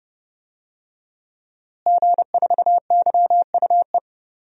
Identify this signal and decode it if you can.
Morse code, heard as a keyed tone, about 30 wpm, 700 Hz. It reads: G4YUE
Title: DXCC-100 Call Signs - 5 Characters - Repeated 2x - 30wpm